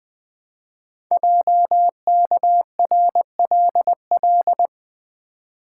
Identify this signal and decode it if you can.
Morse code, heard as a keyed tone, about 20 wpm, 700 Hz. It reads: JKRLL
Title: Code Groups - 5 Letters - 20wpm